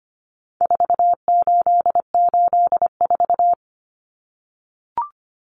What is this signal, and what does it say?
Morse code, heard as a keyed tone, about 25 wpm, 700 Hz. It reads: 4884
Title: Common Cut Numbers - 4-Digit 25wpm